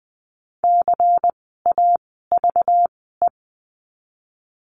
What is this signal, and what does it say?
Morse code, heard as a keyed tone, about 20 wpm, spaced as for 15 wpm, 700 Hz. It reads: CAVE
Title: 192 Lesson 31f - Letter v - Words 20wpm